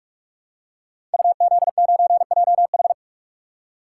Morse code, heard as a keyed tone, about 44 wpm, 700 Hz. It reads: UZ9JH
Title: Code Groups - 5 Letter:Numbers - 45wpm